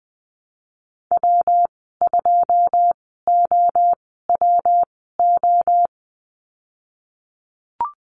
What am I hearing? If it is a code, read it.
Morse code, heard as a keyed tone, about 20 wpm, spaced as for 14 wpm, 700 Hz. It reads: W2OWO
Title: 173 Lesson 28g - Number 9 - Call Signs 20wpm